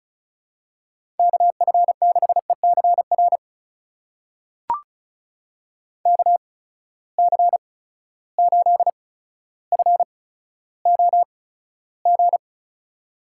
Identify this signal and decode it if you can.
Morse code, heard as a keyed tone, about 36 wpm, spaced as for 15 wpm, 700 Hz. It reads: KF6ECR KC8FOG
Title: USA Call Signs - 6 Characters - 8x Character Spacing - 35wpm